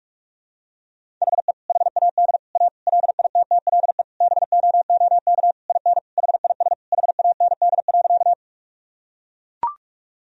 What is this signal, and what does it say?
Morse code, heard as a keyed tone, about 42 wpm, 700 Hz. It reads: HE HAD A LITTLE BOOK IN HIS HAND.
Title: Sentences from Top 1000 Words - Review Part 1 - 45wpm